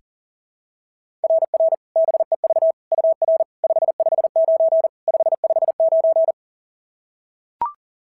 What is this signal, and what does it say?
Morse code, heard as a keyed tone, about 42 wpm, 690 Hz. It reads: RR BEV UR 559 559
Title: SOTA QSO Practice 40wpm